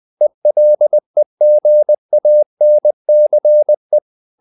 Morse code, heard as a keyed tone, about 20 wpm, 600 Hz. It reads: ELEGANCE
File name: the old man and the sea / the old man and the sea part 1 20wpm